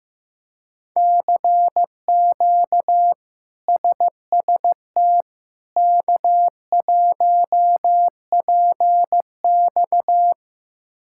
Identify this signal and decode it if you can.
Morse code, heard as a keyed tone, about 14 wpm, 700 Hz. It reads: CQ SST K1PX
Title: K1USN SST QSO Practice - 15wpm